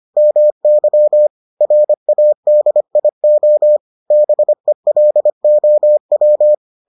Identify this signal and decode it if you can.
Morse code, heard as a keyed tone, about 25 wpm, 600 Hz. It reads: MY RADIO BELOW